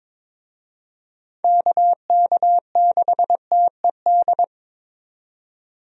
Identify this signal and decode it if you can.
Morse code, heard as a keyed tone, about 23 wpm, 700 Hz. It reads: KK6TED
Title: USA Call Signs - 6 Characters - 22wpm